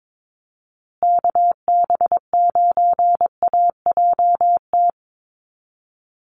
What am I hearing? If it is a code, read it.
Morse code, heard as a keyed tone, about 22 wpm, 700 Hz. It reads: KB9AJT
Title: USA Call Signs Repeated 3x 22wpm